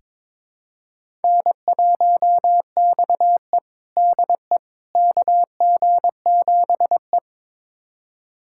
Morse code, heard as a keyed tone, about 22 wpm, 700 Hz. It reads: N1XE DE KG7E
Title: USA Callsign DE Callsign - 4 Characters - 22wpm